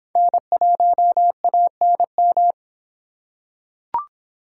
Morse code, heard as a keyed tone, about 26 wpm, 700 Hz. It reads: N1ANM